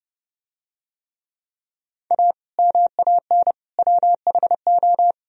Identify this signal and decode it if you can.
Morse code, heard as a keyed tone, about 28 wpm, 700 Hz. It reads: A MAN WHO